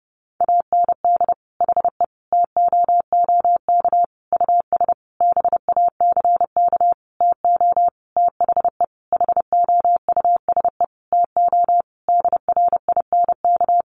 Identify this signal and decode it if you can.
Morse code, heard as a keyed tone, about 30 wpm, 700 Hz. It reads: AND HE TOOK US BACK TO THE HOUSE TO DRINK
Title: Sentences from Top 1500 Words 30wpm